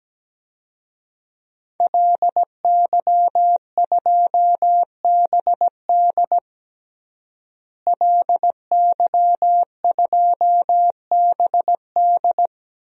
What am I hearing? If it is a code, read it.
Morse code, heard as a keyed tone, about 17 wpm, 700 Hz. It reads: LY2BD LY2BD
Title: DXCC-100 Call Signs - 5 Characters - Repeated 3x - 17wpm